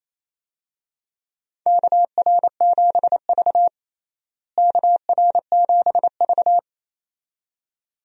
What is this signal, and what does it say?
Morse code, heard as a keyed tone, about 27 wpm, 700 Hz. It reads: KR7V KR7V
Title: USA Call Signs - 4 Characters - Repeated 2x - 28wpm